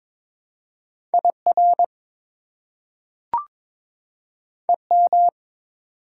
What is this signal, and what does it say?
Morse code, heard as a keyed tone, about 22 wpm, 700 Hz. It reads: IR EM
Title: Prefixes and Suffixes 22wpm